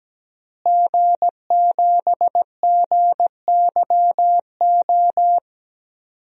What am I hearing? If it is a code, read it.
Morse code, heard as a keyed tone, about 17 wpm, 700 Hz. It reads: G7GYO
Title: DXCC-100 Call Signs - 5 Characters - Repeated 2x - 17wpm